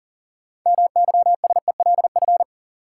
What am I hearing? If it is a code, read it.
Morse code, heard as a keyed tone, about 40 wpm, 700 Hz. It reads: MYSELF
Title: Sentences from Top 1300 Words - New Word Drill - 40wpm